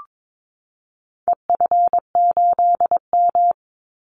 Morse code, heard as a keyed tone, about 22 wpm, 700 Hz. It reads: EF8M
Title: Call Signs Repeated 2x - 22wpm